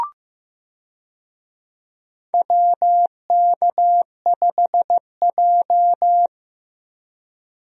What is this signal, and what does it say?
Morse code, heard as a keyed tone, about 15 wpm, 700 Hz. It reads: WK5J
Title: USA Call Signs - 4 Characters - Repeated 2x - 15wpm